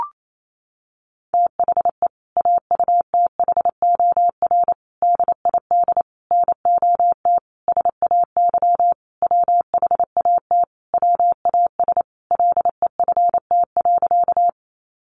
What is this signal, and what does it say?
Morse code, heard as a keyed tone, about 28 wpm, 700 Hz. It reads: THE AUTHOR DID NOT SAY WHAT WAS LEFT.
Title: Sentences from Top 1200 Words 28wpm